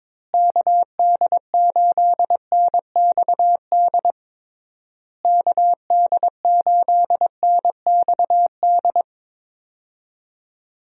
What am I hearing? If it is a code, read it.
Morse code, heard as a keyed tone, about 21 wpm, 700 Hz. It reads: KD8NXD KD8NXD
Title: USA Call Signs - 6 Characters - Repeated 3x - 22wpm